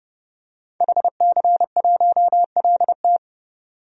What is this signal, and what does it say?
Morse code, heard as a keyed tone, about 30 wpm, 700 Hz. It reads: HC1LT
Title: DXCC-100 Call Signs - 5 Characters - 30wpm